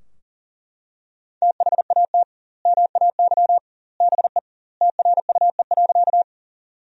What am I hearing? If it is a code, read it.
Morse code, heard as a keyed tone, about 40 wpm, 700 Hz. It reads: THAT MAY BE TRUE.
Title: 2x Word-Spacing - Sentences from Top 400 Words - 40wpm